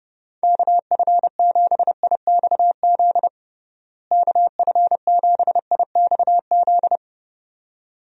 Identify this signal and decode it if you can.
Morse code, heard as a keyed tone, about 30 wpm, 700 Hz. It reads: KF7IXZ KF7IXZ
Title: USA Call Signs - 6 Characters - Repeated 2x - 30wpm